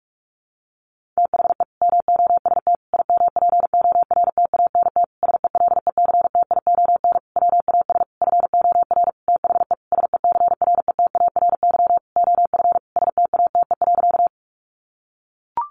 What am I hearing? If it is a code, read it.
Morse code, heard as a keyed tone, about 44 wpm, 700 Hz. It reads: THE MOST IMPORTANT SELECTION WAS FOR THE SECRETARY OF STATE.
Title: Sentences from Top 1400 Words 45wpm